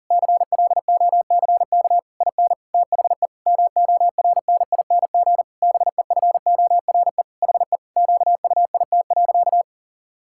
Morse code, heard as a keyed tone, about 40 wpm, 700 Hz. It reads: CLOCK IN THE MORNING BEFORE HE QUIT.
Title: Pangram Challenge v2 - 40wpm